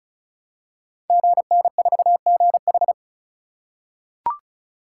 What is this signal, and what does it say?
Morse code, heard as a keyed tone, about 36 wpm, 700 Hz. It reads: GN4GH
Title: Code Groups - 5 Letter:Numbers - 35wpm